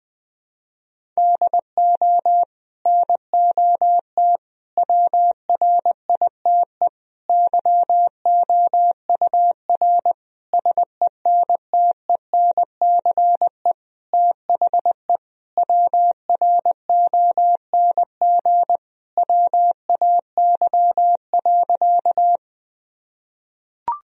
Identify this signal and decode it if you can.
Morse code, heard as a keyed tone, about 20 wpm, 700 Hz. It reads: DO NOT WRITE YOUR SENTENCE THE WRONG WAY.
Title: Sentences from Top 1000 Words - Review Part 1 - 20wpm